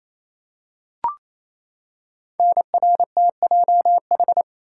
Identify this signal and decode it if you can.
Morse code, heard as a keyed tone, about 28 wpm, 700 Hz. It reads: NRTJH
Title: Code Groups - 5 Letters - 28wpm